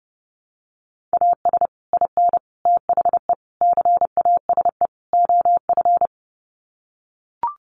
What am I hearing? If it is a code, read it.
Morse code, heard as a keyed tone, about 30 wpm, 700 Hz. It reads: AS IN THE CASE OF
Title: Sets of 5 Words 30wpm